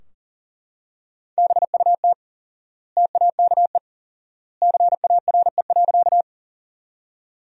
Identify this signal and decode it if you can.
Morse code, heard as a keyed tone, about 40 wpm, 700 Hz. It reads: BUT TAKE CARE.
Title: 4x Word-Spacing - Sentences from Top 300 Words - 40wpm